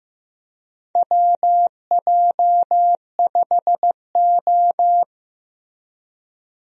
Morse code, heard as a keyed tone, about 15 wpm, 700 Hz. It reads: WJ5O
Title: USA Call Signs - 4 Characters - Repeated 3x - 15wpm